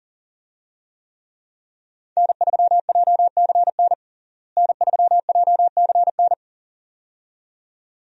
Morse code, heard as a keyed tone, about 40 wpm, 700 Hz. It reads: N3JCN N3JCN
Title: USA Call Signs - 5 Characters - Repeated 2x - 40wpm